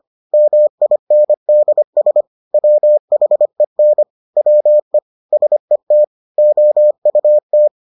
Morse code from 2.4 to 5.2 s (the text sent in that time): WHEN WE